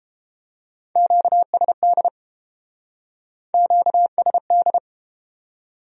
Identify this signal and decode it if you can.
Morse code, heard as a keyed tone, about 29 wpm, 700 Hz. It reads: QSD QSD